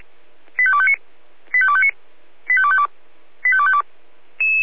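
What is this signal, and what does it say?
Five-tone selective call (ZVEI), heard as quick runs of five tones, tones of about 70 ms. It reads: 85259 85259 85252 85252
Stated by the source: ZVEI five-tone selective call 85232 received via SDR.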